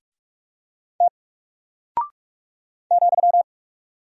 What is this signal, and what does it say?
Morse code, heard as a keyed tone, about 44 wpm, 700 Hz. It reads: T ,
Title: Single Character (Letter:Number:Punctuation:Prosign) Alt 45wpm